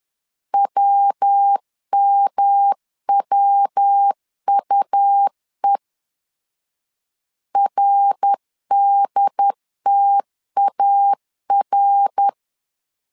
Morse code, heard as a keyed tone, about 10 wpm, 800 Hz. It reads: WMWUE RDTAR
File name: LEZIONI / 3 MWRO / 3 MWR 10wpm 1952